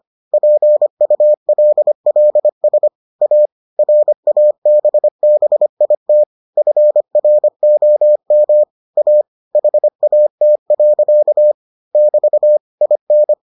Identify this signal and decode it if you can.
Morse code, heard as a keyed tone, about 25 wpm, 600 Hz. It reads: PULLS A RABBIT FROM A HAT. <BT> IN